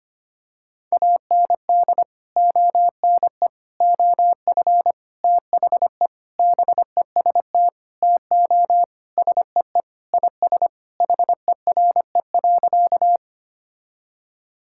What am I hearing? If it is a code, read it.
Morse code, heard as a keyed tone, about 25 wpm, 700 Hz. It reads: AND ONE OF THE BEST TO SEE IS HERE.
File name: practice / 25wpm / Sentences with 4-Word Phrases - 25wpm